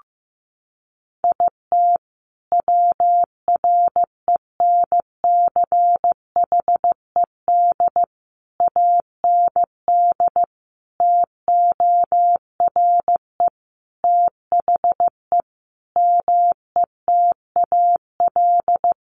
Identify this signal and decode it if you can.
Morse code, heard as a keyed tone, about 15 wpm, 700 Hz. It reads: IT WRENCHED AND TORE THE METAL
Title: A Sound Of Thunder 15wpm